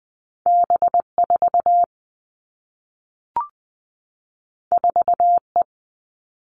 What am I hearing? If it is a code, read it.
Morse code, heard as a keyed tone, about 20 wpm, 700 Hz. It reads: B 4 4 E
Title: Two Challenging Letter-Number Combinations 20wpm